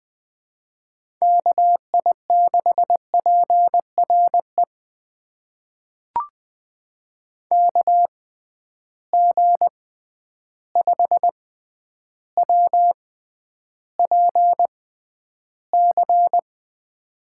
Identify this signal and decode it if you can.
Morse code, heard as a keyed tone, about 20 wpm, spaced as for 10 wpm, 700 Hz. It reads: KI6PRE KG5WPC
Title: USA Call Signs - 6 Characters - 6x Character Spacing - 20wpm